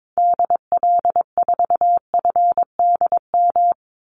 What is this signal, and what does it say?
Morse code, heard as a keyed tone, about 22 wpm, 700 Hz. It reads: DL4FDM
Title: Call Signs Repeated 2x - 22wpm